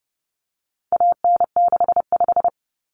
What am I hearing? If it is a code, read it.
Morse code, heard as a keyed tone, about 30 wpm, 700 Hz. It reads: AN65
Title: Common Cut Numbers - 4-Digit 30wpm